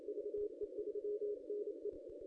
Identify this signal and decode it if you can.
Morse code, heard as a keyed tone, about 27 wpm, 440 Hz. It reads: VE3NE
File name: ss-cw-k5tr-op-n5rz